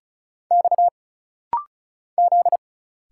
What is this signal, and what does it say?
Morse code, heard as a keyed tone, about 36 wpm, 700 Hz. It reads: X Z
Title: Single Letter-Number Alt - Rapid-Fire 35wpm